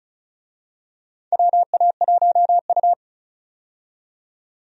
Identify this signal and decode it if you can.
Morse code, heard as a keyed tone, about 36 wpm, 700 Hz. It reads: WA1U
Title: USA Call Signs - 4 Characters - 35wpm